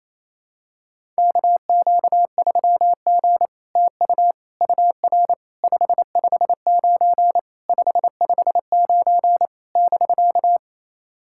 Morse code, heard as a keyed tone, about 27 wpm, 700 Hz. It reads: KQ3G TU UR 559 559 <BK>
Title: SOTA QSO Practice 28wpm